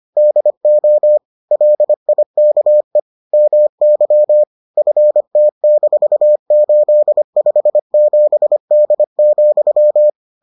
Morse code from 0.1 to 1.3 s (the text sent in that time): DO